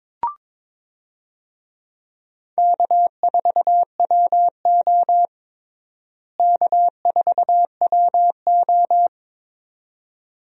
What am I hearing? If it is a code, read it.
Morse code, heard as a keyed tone, about 21 wpm, 700 Hz. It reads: K4WO K4WO
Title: USA Call Signs - 4 Characters - Repeated 2x - 22wpm